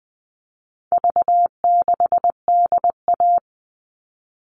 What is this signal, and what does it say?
Morse code, heard as a keyed tone, about 20 wpm, 700 Hz. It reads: V6DA
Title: All Cut Numbers - 4-Digit 20wpm